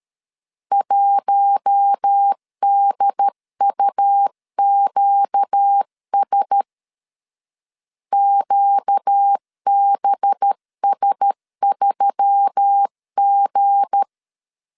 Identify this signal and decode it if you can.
Morse code, heard as a keyed tone, about 13 wpm, 800 Hz. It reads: 1DUQS QBS3G
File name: LEZIONI / 7 GF9Z / 7 G 12wpm 2055